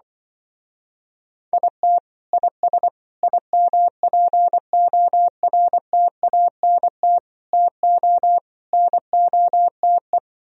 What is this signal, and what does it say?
Morse code, heard as a keyed tone, about 24 wpm, 700 Hz. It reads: IT IS IMPORTANT TO NOTE
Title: Speed-Racing - Sets of 5 Words 20wpm